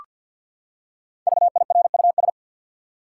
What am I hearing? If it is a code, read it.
Morse code, heard as a keyed tone, about 53 wpm, 700 Hz. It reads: VIRUS
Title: Five Letter Words - 50wpm